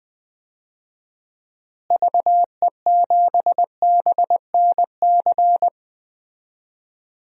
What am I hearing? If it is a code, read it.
Morse code, heard as a keyed tone, about 20 wpm, 700 Hz. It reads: VE7BNC
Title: DXCC-100 Call Signs - 6 Characters - Repeated 2x - 20wpm